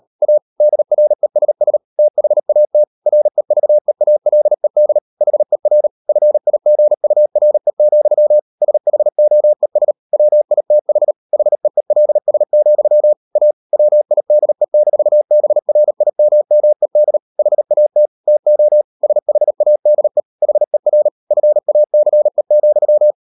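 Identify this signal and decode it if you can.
Morse code, heard as a keyed tone, about 38 wpm, 600 Hz. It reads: A DRESS THAT REVEALED HER FIGURE, SHOES WITH HEELS, A WIDE-BRIMMED HAT TO SHADE HER FACE,
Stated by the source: Morse code practice at 38 WPM, sending Ken Follett's novel Never in English, from CW med Gnister.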